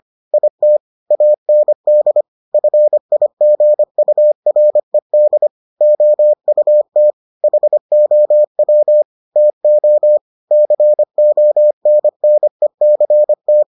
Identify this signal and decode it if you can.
Morse code, heard as a keyed tone, about 25 wpm, 600 Hz. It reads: IT AND FIGURED OUT HOW TO CONNECT